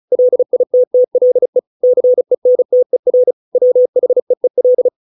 Morse code, heard as a keyed tone, about 34 wpm, 500 Hz. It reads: LITTLE CENTER WHEEL